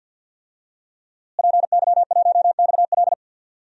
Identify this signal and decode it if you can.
Morse code, heard as a keyed tone, about 47 wpm, 700 Hz. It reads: PY1XL
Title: DXCC-100 Call Signs - 5 Characters - Repeated 2x - 50wpm